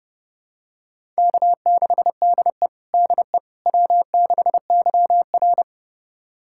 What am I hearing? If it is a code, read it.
Morse code, heard as a keyed tone, about 30 wpm, 700 Hz. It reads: K6DE DE W6YR
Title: USA Callsign DE Callsign - 4 Characters - 30wpm